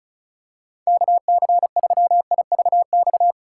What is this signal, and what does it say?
Morse code, heard as a keyed tone, about 34 wpm, 700 Hz. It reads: KC3IVX
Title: USA Call Signs - 6 Characters - 8x Character Spacing - 35wpm